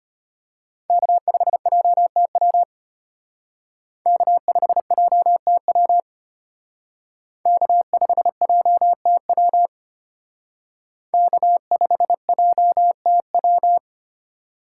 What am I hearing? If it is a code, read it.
Morse code, heard as a keyed tone, about 33 wpm, 700 Hz. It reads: K5JTW K5JTW K5JTW K5JTW